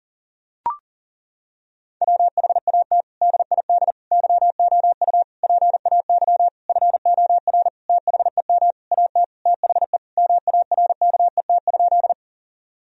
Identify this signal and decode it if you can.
Morse code, heard as a keyed tone, about 40 wpm, 700 Hz. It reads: WHAT DID YOU PAY FOR THEM AT THE MARKET?
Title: Sentences from Top 1000 Words - Review Part 3 - 40wpm